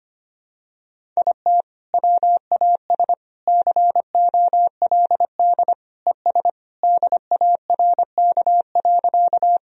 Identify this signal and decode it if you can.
Morse code, heard as a keyed tone, about 25 wpm, 700 Hz. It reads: IT WAS COLD ES DARK.
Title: Sentences from Top 500 Words and Abbreviations 25wpm